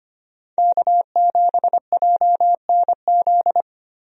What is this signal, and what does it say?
Morse code, heard as a keyed tone, about 25 wpm, 700 Hz. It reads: K7JNZ
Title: USA Call Signs - 5 Characters - Repeated 3x - 25wpm